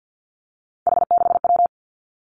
Morse code, heard as a keyed tone, about 50 wpm, 700 Hz. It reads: H6W
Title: Three Challenging Letter-Number Combinations 50wpm